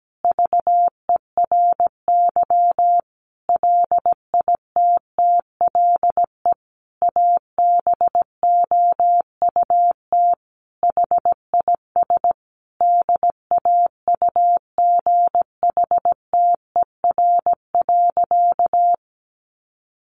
Morse code, heard as a keyed tone, about 17 wpm, 700 Hz. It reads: VERY LITTLE ABOUT HIS DAUGHTER.
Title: Sentences from Top 1500 Words 17wpm